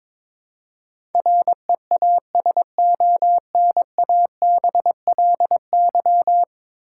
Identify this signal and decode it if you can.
Morse code, heard as a keyed tone, about 22 wpm, 700 Hz. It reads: REASONABLY
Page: morse-code-ninja-practice-22wpm